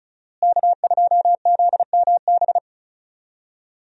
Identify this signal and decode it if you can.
Morse code, heard as a keyed tone, about 36 wpm, 700 Hz. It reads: K2ZMB
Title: USA Call Signs - 5 Characters - Repeated 3x - 35wpm